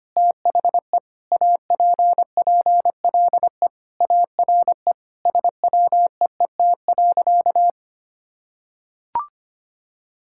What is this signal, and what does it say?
Morse code, heard as a keyed tone, about 25 wpm, 700 Hz. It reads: THE APPLE ARE SWEET.
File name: practice / 25wpm / Sentences with 3-Word Phrases - 25wpm